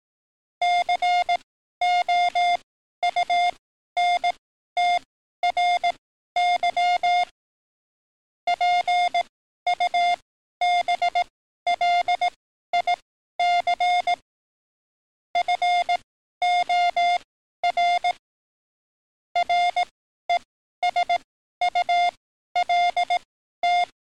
Morse code, heard as a keyed tone, about 18 wpm, spaced as for 12 wpm, 710 Hz. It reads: COUNTRY PUBLIC FOR RESULT
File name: training / NAVPERS 11049to11060 / NAVPERS 11053 RA RB